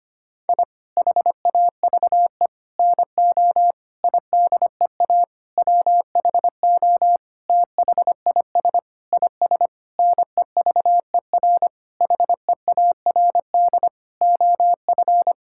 Morse code, heard as a keyed tone, about 25 wpm, 700 Hz. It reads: I HAVE NO IDEA WHO THIS IS NEVER HEARD OF
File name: practice / 25wpm / Sentences with 9-Word Phrases - 25wpm